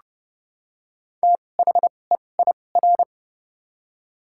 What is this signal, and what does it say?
Morse code, heard as a keyed tone, about 30 wpm, spaced as for 22 wpm, 700 Hz. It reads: THEIR
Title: 2x Character Spacing - Top 200 Words - 30wpm